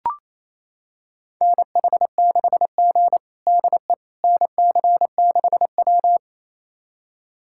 Morse code, heard as a keyed tone, about 28 wpm, 700 Hz. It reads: NH6G DE NC6W
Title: USA Callsign DE Callsign - 4 Characters - 28wpm